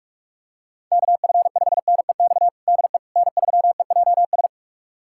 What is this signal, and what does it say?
Morse code, heard as a keyed tone, about 46 wpm, 700 Hz. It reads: KF5NEX DE N3EJS
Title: USA Callsign DE Callsign 45wpm